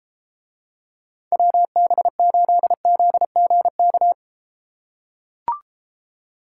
Morse code, heard as a keyed tone, about 32 wpm, 700 Hz. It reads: WB8ZGK